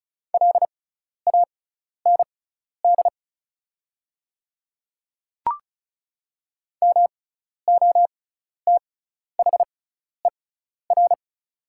Morse code, heard as a keyed tone, about 34 wpm, spaced as for 12 wpm, 700 Hz. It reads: LAND MOTHER
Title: 6x Character Spacing - Top 200 Words - 35wpm